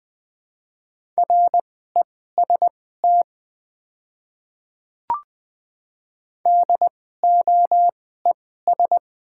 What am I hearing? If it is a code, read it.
Morse code, heard as a keyed tone, about 20 wpm, spaced as for 15 wpm, 690 Hz. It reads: REST DOES
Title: 2x Character Spacing - Top 500 Words - 20wpm